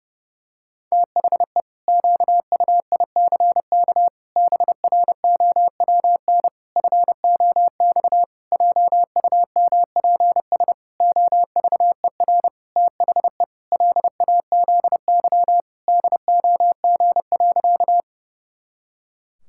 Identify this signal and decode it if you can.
Morse code, heard as a keyed tone, about 30 wpm, 700 Hz. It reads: THE QUICK BROWN FOX JUMPS OVER THE LAZY DOG.
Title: Pangram Challenge v2 - 30wpm